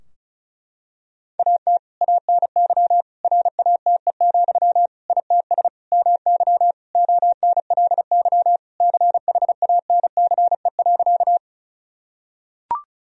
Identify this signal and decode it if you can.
Morse code, heard as a keyed tone, about 36 wpm, 700 Hz. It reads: AT ANY RATE, ITS MY ONLY CHANCE.
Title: Sentences from Top 1000 Words - Review Part 4 - 35wpm